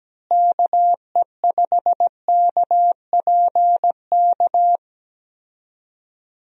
Morse code, heard as a keyed tone, about 17 wpm, 700 Hz. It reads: KE5KPK
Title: USA Call Signs - 6 Characters - Repeated 2x - 17wpm